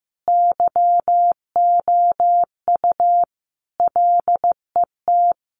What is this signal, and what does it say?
Morse code, heard as a keyed tone, about 15 wpm, 700 Hz. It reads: YOU LET
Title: Sentences from Top 300 Words 15wpm